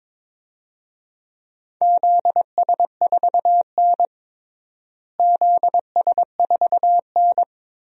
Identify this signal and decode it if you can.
Morse code, heard as a keyed tone, about 22 wpm, 700 Hz. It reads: ZS4N ZS4N
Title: DXCC-100 Call Signs - 4 Characters - Repeated 3x - 22wpm